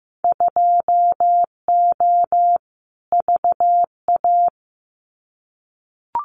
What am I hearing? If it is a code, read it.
Morse code, heard as a keyed tone, about 15 wpm, 700 Hz. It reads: 2O VA
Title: Winter Field Day Exchanges 15wpm